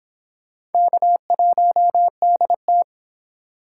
Morse code, heard as a keyed tone, about 31 wpm, 700 Hz. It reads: K1DT